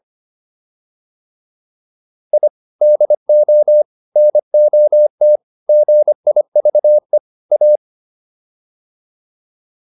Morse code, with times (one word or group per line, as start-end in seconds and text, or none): 2.3-2.5 I
2.8-3.8 DO
4.2-5.4 NOT
5.7-7.2 GIVE
7.5-7.8 A